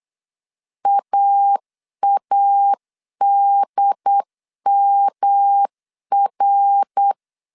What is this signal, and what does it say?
Morse code, heard as a keyed tone, about 8 wpm, 800 Hz. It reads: AADMR
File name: CW / DET / 1 6 1500C S C 8wpm 5219